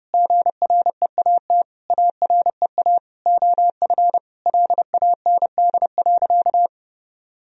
Morse code, heard as a keyed tone, about 30 wpm, 700 Hz. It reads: GREAT AREA OF LAND.
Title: Sentences from Top 1000 Words - Review Part 2 - 30wpm